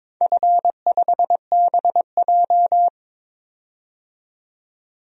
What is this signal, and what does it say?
Morse code, heard as a keyed tone, about 22 wpm, 700 Hz. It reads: F5BJ
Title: DXCC-100 Call Signs - 4 Characters - 22wpm